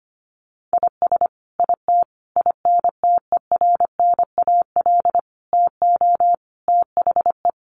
Morse code, heard as a keyed tone, about 25 wpm, 700 Hz. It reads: IS IT INTERNAL TO THE
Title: Sentences from Top 1400 Words 25wpm